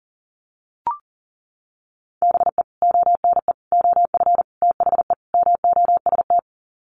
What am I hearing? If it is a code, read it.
Morse code, heard as a keyed tone, about 40 wpm, 700 Hz. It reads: BE ONE OF THE MOST
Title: Sets of 5 Words Encore 40wpm